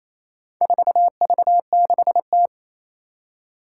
Morse code, heard as a keyed tone, about 28 wpm, 700 Hz. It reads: 4V6T
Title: Four Challenging Letter-Number Combinations 28wpm